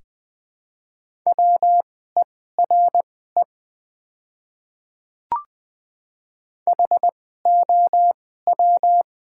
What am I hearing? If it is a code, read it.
Morse code, heard as a keyed tone, about 20 wpm, spaced as for 14 wpm, 700 Hz. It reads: WERE HOW
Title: 2x Character Spacing - Top 100 Words - 20wpm